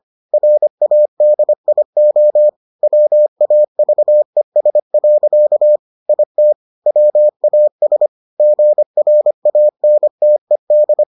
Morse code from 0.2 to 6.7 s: RADIO WAVES. IT

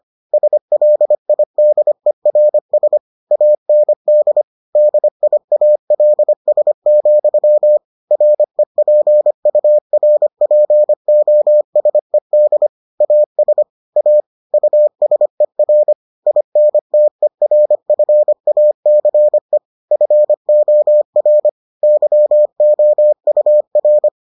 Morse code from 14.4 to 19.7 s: USER INTERFACE

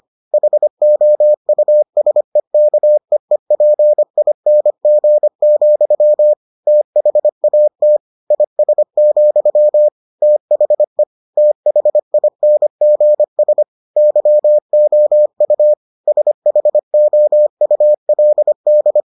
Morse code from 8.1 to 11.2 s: IS, THE

